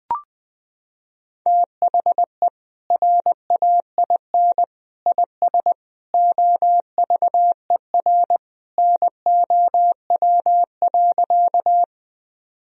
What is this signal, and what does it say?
Morse code, heard as a keyed tone, about 20 wpm, 700 Hz. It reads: THE RAIN IS OVER NOW.
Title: Sentences from Top 1000 Words - Review Part 1 - 20wpm